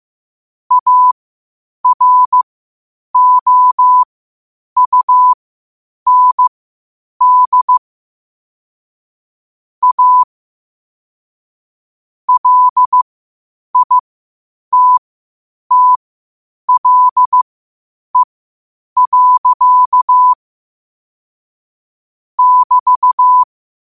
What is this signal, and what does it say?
Morse code, heard as a keyed tone, about 15 wpm, spaced as for 8 wpm, 1000 Hz. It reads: AROUND A LITTLE. <BT>
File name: qotdfiles / 8wpm